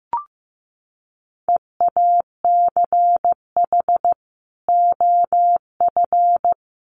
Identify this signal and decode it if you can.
Morse code, heard as a keyed tone, about 14 wpm, 700 Hz. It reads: EACH OF
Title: Sentences from Top 500 Words 15wpm